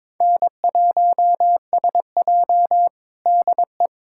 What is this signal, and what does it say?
Morse code, heard as a keyed tone, about 22 wpm, 700 Hz. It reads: N1SJ DE
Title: USA Callsign DE Callsign - 4 Characters - 22wpm